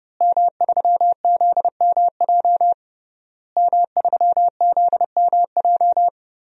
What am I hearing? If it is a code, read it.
Morse code, heard as a keyed tone, about 30 wpm, 700 Hz. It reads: M3ZMJ M3ZMJ
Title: DXCC-100 Call Signs - 5 Characters - Repeated 3x - 30wpm